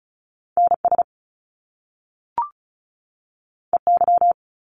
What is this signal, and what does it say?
Morse code, heard as a keyed tone, about 36 wpm, 700 Hz. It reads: NS EY
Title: Top Two Letter Combinations in Words 35wpm